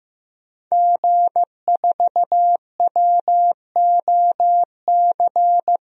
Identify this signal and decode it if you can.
Morse code, heard as a keyed tone, about 15 wpm, 700 Hz. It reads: G4WOC
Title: DXCC-100 Call Signs - 5 Characters - 15wpm